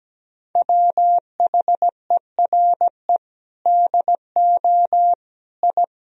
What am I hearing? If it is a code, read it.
Morse code, heard as a keyed tone, about 17 wpm, 700 Hz. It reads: WHERE DO I
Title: Sentences from Top 1000 Words - Review Part 1 - 17wpm